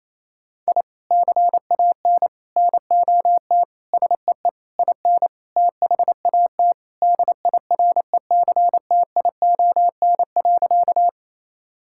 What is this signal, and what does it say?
Morse code, heard as a keyed tone, about 28 wpm, 700 Hz. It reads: I CAN NOT SEE IN THAT DIRECTION.
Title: Sentences from Top 1000 Words - Review Part 5 - 28wpm